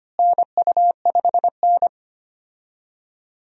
Morse code, heard as a keyed tone, about 25 wpm, 700 Hz. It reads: NU5N
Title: USA Call Signs - 4 Characters - Repeated 3x - 25wpm